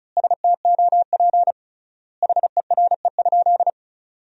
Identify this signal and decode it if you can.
Morse code, heard as a keyed tone, about 36 wpm, 700 Hz. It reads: STOP HERE?
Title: 3x Word-Spacing - Sentences from Top 300 Words - 35wpm